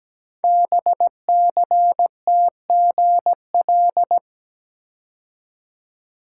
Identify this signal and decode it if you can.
Morse code, heard as a keyed tone, about 17 wpm, 700 Hz. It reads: BCTGL
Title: Code Groups - 5 Letters - 17wpm